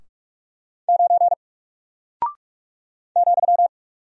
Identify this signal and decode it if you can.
Morse code, heard as a keyed tone, about 44 wpm, 700 Hz. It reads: 9 ,
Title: Single Character (Letter:Number:Punctuation:Prosign) Alt 45wpm